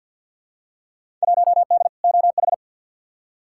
Morse code, heard as a keyed tone, about 47 wpm, 700 Hz. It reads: 1D OH
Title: Field Day Exchanges - 50wpm